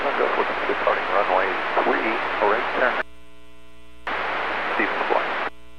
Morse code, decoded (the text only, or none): none